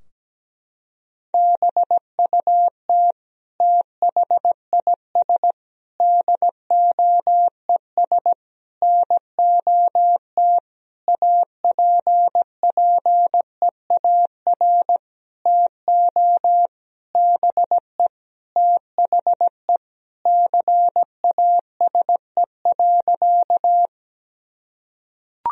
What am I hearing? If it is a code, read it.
Morse code, heard as a keyed tone, about 17 wpm, 700 Hz. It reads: BUT THIS DOES NOT APPEAR TO BE THE CASE.
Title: Sentences from Top 1000 Words - Review Part 2 - 17wpm